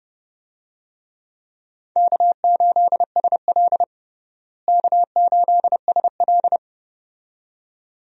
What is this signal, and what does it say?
Morse code, heard as a keyed tone, about 30 wpm, 700 Hz. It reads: K8SL K8SL
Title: USA Call Signs - 4 Characters - Repeated 2x - 30wpm